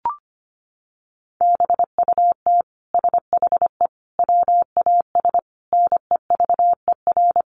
Morse code, heard as a keyed tone, about 25 wpm, 700 Hz. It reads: BUT SHE WAS NEVER
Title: Sentences from Top 700 Words 25wpm